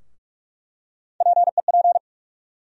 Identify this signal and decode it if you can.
Morse code, heard as a keyed tone, about 44 wpm, 700 Hz. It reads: PEP
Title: USA Call Sign Postfixes - 3 Letters 45wpm